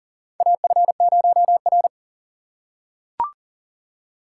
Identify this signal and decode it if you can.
Morse code, heard as a keyed tone, about 40 wpm, 700 Hz. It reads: AF0R
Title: USA Call Signs - 4 Characters - 2x Character Spacing - 40wpm